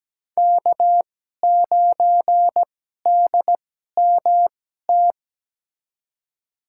Morse code, heard as a keyed tone, about 17 wpm, spaced as for 12 wpm, 700 Hz. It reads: K9DMT
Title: USA Call Signs - 5 Characters - 2x Character Spacing - 17wpm